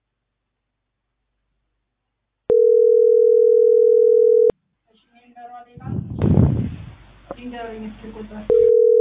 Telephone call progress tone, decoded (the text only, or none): ringback tone